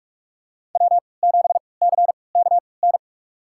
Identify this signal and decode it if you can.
Morse code, heard as a keyed tone, about 44 wpm, spaced as for 25 wpm, 700 Hz. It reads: W7CKN
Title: USA Call Signs - 3x Character Spacing - 45wpm